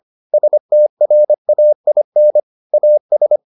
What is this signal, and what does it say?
Morse code, heard as a keyed tone, about 25 wpm, 600 Hz. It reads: STRAIN AS